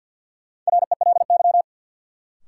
Morse code, heard as a keyed tone, about 51 wpm, 700 Hz. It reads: RELY
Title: Sentences from Top 1200 Words - New Word Drill Alt - 50wpm